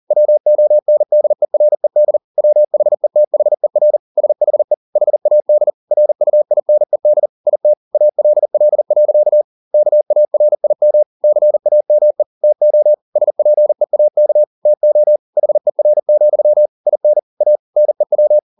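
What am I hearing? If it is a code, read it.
Morse code, heard as a keyed tone, about 40 wpm, 600 Hz. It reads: WONDERED WHETHER SHE HAD RUINED IT ALL. KARIM CAME TO SPEAK TO HER, IN A NEW
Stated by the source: Morse code at 40 wpm, practice text, CW med Gnister.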